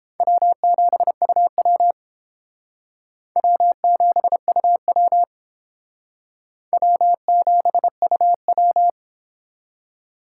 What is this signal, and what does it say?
Morse code, heard as a keyed tone, about 30 wpm, 700 Hz. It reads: W7UW W7UW W7UW